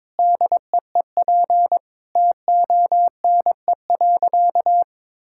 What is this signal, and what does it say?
Morse code, heard as a keyed tone, about 21 wpm, 700 Hz. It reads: DEEP TONE.
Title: Sentences from Top 1600 Words 22wpm